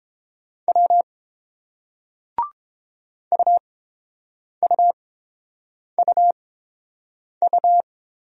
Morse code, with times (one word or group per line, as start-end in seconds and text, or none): 0.7-1.0 W
3.3-7.8 UUUU